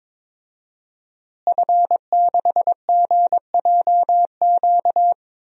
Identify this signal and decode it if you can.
Morse code, heard as a keyed tone, about 22 wpm, 700 Hz. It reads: F6GJQ
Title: DXCC-100 Call Signs - 5 Characters - Repeated 2x - 22wpm